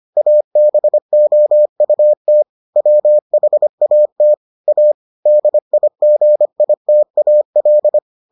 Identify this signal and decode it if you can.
Morse code, heard as a keyed tone, about 25 wpm, 600 Hz. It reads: ABOUT WHAT A DIGITAL